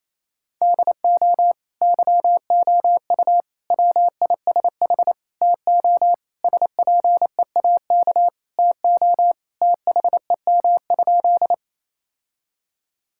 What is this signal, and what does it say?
Morse code, heard as a keyed tone, about 28 wpm, 700 Hz. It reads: DO YOU WISH TO SPEAK TO THEM?
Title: Sentences from Top 1600 Words 28wpm